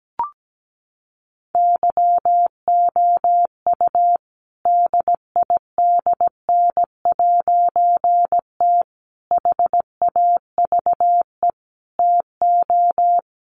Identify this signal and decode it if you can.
Morse code, heard as a keyed tone, about 17 wpm, 700 Hz. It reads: YOU DIDN'T HAVE TO
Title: A Sound Of Thunder 17wpm